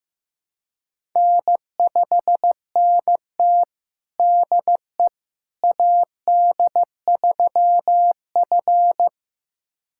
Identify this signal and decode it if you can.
Morse code, heard as a keyed tone, about 15 wpm, 700 Hz. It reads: N5NT DE AD3F
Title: USA Callsign DE Callsign - 4 Characters - 15wpm